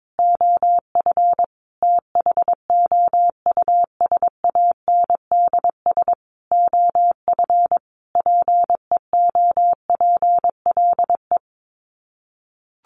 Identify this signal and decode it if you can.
Morse code, heard as a keyed tone, about 22 wpm, 700 Hz. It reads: OF THOUSANDS OF PEOPLE
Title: Sets of 5 Words Encore2 22wpm